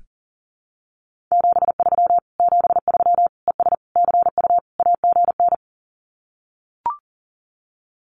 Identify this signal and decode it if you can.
Morse code, heard as a keyed tone, about 40 wpm, 700 Hz. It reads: 73 73 ES CU AGN
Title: Common QSO Phrases Encore 40wpm